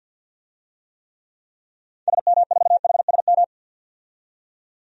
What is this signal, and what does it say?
Morse code, heard as a keyed tone, about 51 wpm, 700 Hz. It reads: SM4HSM
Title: DXCC-100 Call Signs - 6 Characters - 50wpm